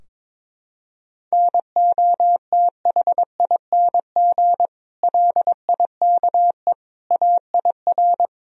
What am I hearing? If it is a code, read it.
Morse code, heard as a keyed tone, about 21 wpm, 700 Hz. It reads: NOTHING LIKE AIR
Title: Sentences from Top 1000 Words - Review Part 4 - 22wpm